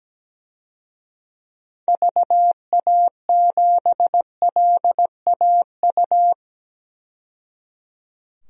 Morse code, heard as a keyed tone, about 17 wpm, 700 Hz. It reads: VA7LAU
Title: DXCC-100 Call Signs - 6 Characters - 17wpm